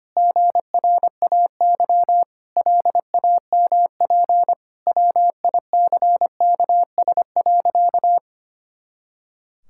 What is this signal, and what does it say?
Morse code, heard as a keyed tone, about 25 wpm, 700 Hz. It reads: GRAY LAMP WICKS.
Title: Pangram Challenge v2 - 25wpm